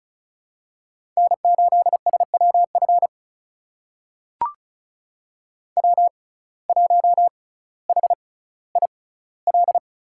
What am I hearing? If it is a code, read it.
Morse code, heard as a keyed tone, about 36 wpm, spaced as for 18 wpm, 700 Hz. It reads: N8SWF W1HIL
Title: USA Call Signs - 5 Characters - 6x Character Spacing - 35wpm